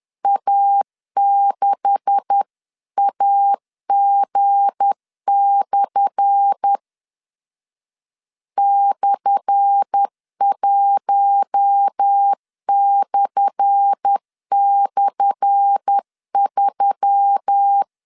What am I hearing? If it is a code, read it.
Morse code, heard as a keyed tone, about 10 wpm, 800 Hz. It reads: A6AG/ /1//3
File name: LEZIONI / 10 SIGNS / 10A SLASH 10wpm 2808